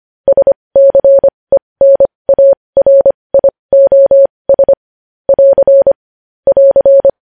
Morse code, heard as a keyed tone, about 25 wpm, spaced as for 20 wpm, 550 Hz. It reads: SCENARIOS <AR> <AR>
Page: News headlines in Morse code at 20 WPM